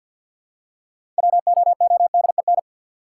Morse code, heard as a keyed tone, about 52 wpm, 700 Hz. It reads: WOODEN
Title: Six Letter Words - 50wpm